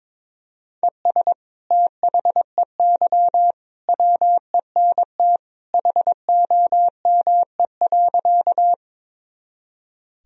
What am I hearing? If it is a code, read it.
Morse code, heard as a keyed tone, about 22 wpm, 700 Hz. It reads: ES THEY WENT HOME.
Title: Sentences from Top 200 Words and Abbreviations 22wpm